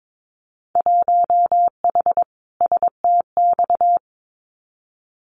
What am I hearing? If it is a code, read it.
Morse code, heard as a keyed tone, about 22 wpm, 700 Hz. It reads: 1H STX
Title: Winter Field Day Exchanges Encore 22wpm